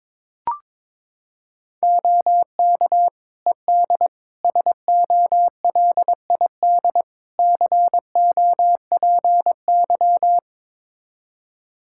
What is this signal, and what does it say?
Morse code, heard as a keyed tone, about 22 wpm, 700 Hz. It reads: OK ED SOLID COPY
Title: Practice QSO - Exchange 3 - More Info - 22wpm